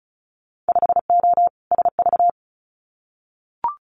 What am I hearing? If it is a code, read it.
Morse code, heard as a keyed tone, about 36 wpm, 700 Hz. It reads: 5O SV
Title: Winter Field Day Exchanges Encore 35wpm